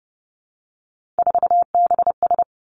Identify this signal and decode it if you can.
Morse code, heard as a keyed tone, about 30 wpm, 700 Hz. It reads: 4BS
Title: Three Challenging Letter-Number Combinations 30wpm